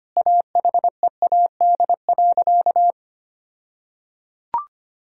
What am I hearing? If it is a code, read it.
Morse code, heard as a keyed tone, about 25 wpm, 700 Hz. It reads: AHEAD.
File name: practice / 25wpm / All-Age Appropriate Jokes - 25wpm